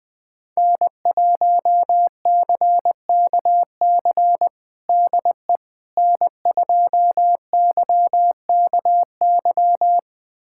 Morse code, heard as a keyed tone, about 20 wpm, 700 Hz. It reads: N1CKC DE N2YKY
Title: USA Callsign DE Callsign - 5 Characters - 20wpm